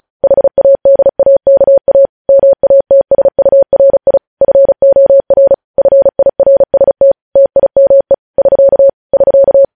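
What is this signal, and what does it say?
Morse code, heard as a keyed tone, about 36 wpm, 550 Hz. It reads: HADAKA MATSURI FOR FIRST TIME <SK> <SK>
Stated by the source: Morse code transmission of news headlines, preceded by vvv.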